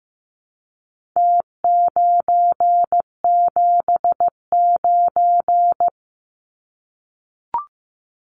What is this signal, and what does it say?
Morse code, heard as a keyed tone, about 15 wpm, 700 Hz. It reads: T979
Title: Numbers - Contest Style - 4-Digit 15wpm